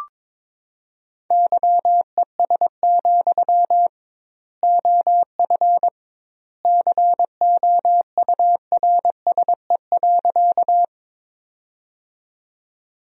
Morse code, heard as a keyed tone, about 22 wpm, 700 Hz. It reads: YES, OF COURSE.
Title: 2x Word-Spacing - Sentences from Top 500 Words - 22wpm